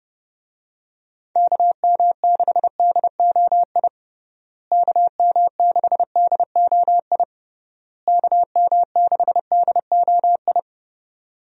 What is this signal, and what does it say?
Morse code, heard as a keyed tone, about 30 wpm, 700 Hz. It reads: KM6DOI KM6DOI KM6DOI
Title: USA Call Signs - 6 Characters - Repeated 3x - 30wpm